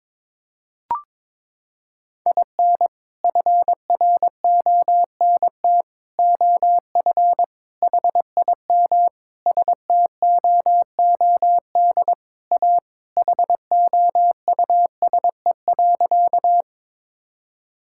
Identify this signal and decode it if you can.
Morse code, heard as a keyed tone, about 22 wpm, 700 Hz. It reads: IN FRONT OF HIM STOOD A HOUSE.
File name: practice / 22wpm / Sentences with 4-Word Phrases - 22wpm